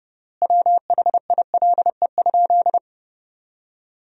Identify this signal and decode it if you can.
Morse code, heard as a keyed tone, about 30 wpm, 700 Hz. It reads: WHILE?
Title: 3x Word-Spacing - Sentences from Top 300 Words - 30wpm